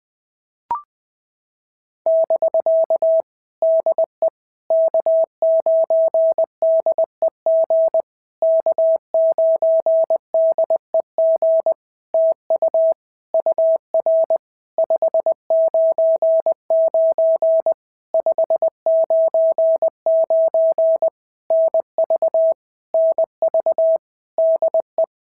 Morse code, heard as a keyed tone, about 20 wpm, 650 Hz. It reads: <BK> DE K9DEG K9DEG TU UR 599 599 NV NV DE
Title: POTA QSO Practice - 20wpm